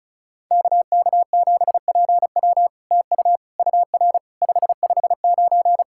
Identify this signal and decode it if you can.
Morse code, heard as a keyed tone, about 36 wpm, 700 Hz. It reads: KK7PW TU UR 559
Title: SOTA QSO Practice 35wpm